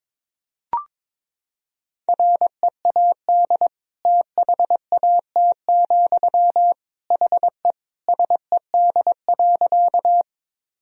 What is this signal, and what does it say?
Morse code, heard as a keyed tone, about 22 wpm, 700 Hz. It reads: READ THAT, HE SED.
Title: Sentences from Top 200 Words and Abbreviations 22wpm